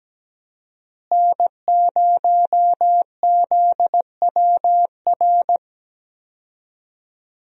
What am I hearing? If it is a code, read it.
Morse code, heard as a keyed tone, about 17 wpm, 700 Hz. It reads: N0ZWR
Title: USA Call Signs - 5 Characters - Repeated 3x - 17wpm